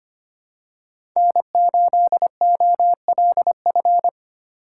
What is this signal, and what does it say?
Morse code, heard as a keyed tone, about 25 wpm, 700 Hz. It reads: N8OLF
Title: USA Call Signs - 5 Characters - 2x Character Spacing - 25wpm